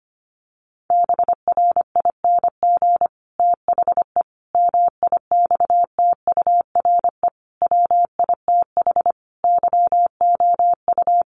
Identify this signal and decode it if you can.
Morse code, heard as a keyed tone, about 25 wpm, 700 Hz. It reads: BRING THE MIXTURE WITH YOU
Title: Sentences from Top 1500 Words 25wpm